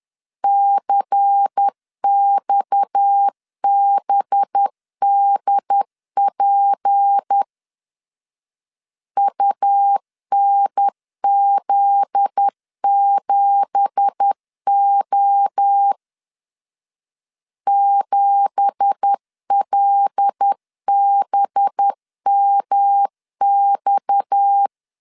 Morse code, heard as a keyed tone, about 10 wpm, 800 Hz. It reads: CXBDP UNZ7O 7LBMX
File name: CW / DET / 1 9 1500C S B 10wpm 4439